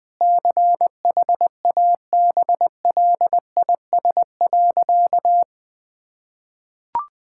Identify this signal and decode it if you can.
Morse code, heard as a keyed tone, about 20 wpm, 700 Hz. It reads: CHABLIS.
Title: Pangram Challenge v2 - 20wpm